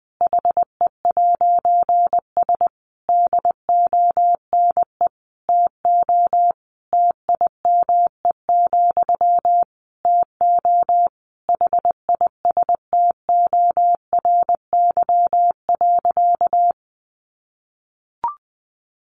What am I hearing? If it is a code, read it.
Morse code, heard as a keyed tone, about 20 wpm, 700 Hz. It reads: HE'S DONE TO TIME, TO HISTORY.
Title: A Sound Of Thunder 20wpm